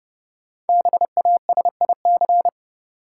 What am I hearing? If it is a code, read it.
Morse code, heard as a keyed tone, about 31 wpm, 700 Hz. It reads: BASIC
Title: Sentences from Top 1000 Words - Review Part 3 - New Word Drill - 30wpm